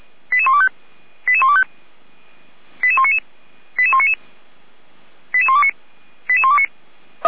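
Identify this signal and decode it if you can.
Five-tone selective call (ZVEI), heard as quick runs of five tones, tones of about 70 ms. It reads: 88125 88125 88199 88199 88129 88129
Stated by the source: ZVEI five-tone selective call received on an SDR.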